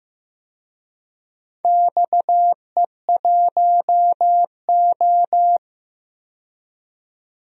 Morse code, heard as a keyed tone, about 15 wpm, 700 Hz. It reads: XE1O
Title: DXCC-100 Call Signs - 4 Characters - 15wpm